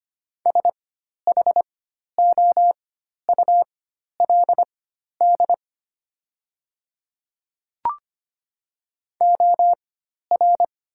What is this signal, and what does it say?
Morse code, heard as a keyed tone, about 25 wpm, spaced as for 12 wpm, 700 Hz. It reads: SHOULD OR
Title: 4x Character Spacing - Top 300 Words - 25wpm